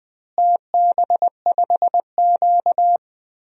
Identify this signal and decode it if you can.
Morse code, heard as a keyed tone, about 19 wpm, 700 Hz. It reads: TB5Q
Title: DXCC-100 Call Signs - 4 Characters - Repeated 3x - 20wpm